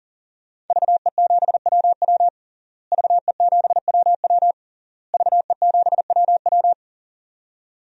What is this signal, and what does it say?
Morse code, heard as a keyed tone, about 40 wpm, 700 Hz. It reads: VE7WW VE7WW VE7WW
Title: DXCC-100 Call Signs - 5 Characters - Repeated 3x - 40wpm